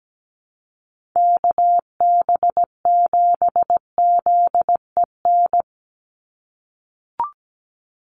Morse code, heard as a keyed tone, about 17 wpm, 700 Hz. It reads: KB7ZEN
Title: USA Call Signs 17wpm